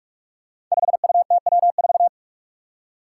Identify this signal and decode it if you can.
Morse code, heard as a keyed tone, about 43 wpm, 700 Hz. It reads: 5UTW4
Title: Five Challenging Letter-Number Combinations 45wpm